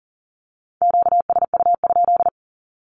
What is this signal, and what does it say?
Morse code, heard as a keyed tone, about 40 wpm, 700 Hz. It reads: QSU?
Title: Q-Codes 40wpm